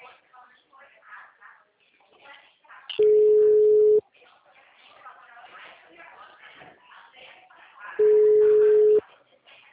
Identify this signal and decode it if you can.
Ringback tone (Europe, 425 Hz), with speech.
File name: out- -2006- - - .25066